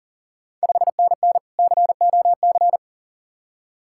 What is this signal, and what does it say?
Morse code, heard as a keyed tone, about 40 wpm, 700 Hz. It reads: 5NN COC
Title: Ontario QSO Party Exchange 40wpm